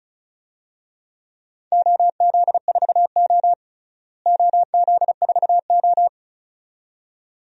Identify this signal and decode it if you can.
Morse code, heard as a keyed tone, about 36 wpm, 700 Hz. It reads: OZ4O OZ4O
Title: DXCC-100 Call Signs - 4 Characters - Repeated 2x - 35wpm